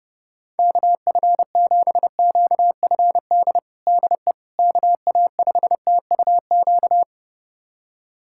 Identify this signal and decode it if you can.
Morse code, heard as a keyed tone, about 30 wpm, 700 Hz. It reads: KF7QFD DE KA5TUQ
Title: USA Callsign DE Callsign - 6 Characters - 30wpm